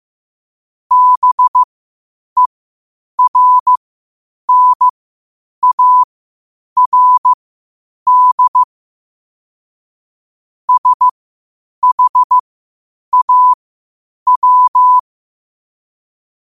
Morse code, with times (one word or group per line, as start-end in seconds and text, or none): 0.9-8.6 BERNARD
10.7-15.0 SHAW